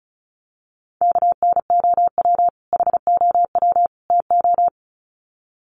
Morse code, heard as a keyed tone, about 34 wpm, 700 Hz. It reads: KNOW HOW TO
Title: Sets of 3 Words 35wpm